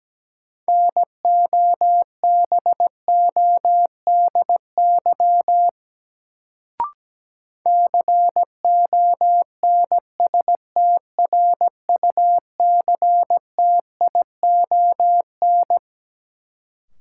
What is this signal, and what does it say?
Morse code, heard as a keyed tone, about 17 wpm, 700 Hz. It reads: NOBODY CONSTRUCTION
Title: Sentences from Top 1600 Words - New Word Drill Alt - 17wpm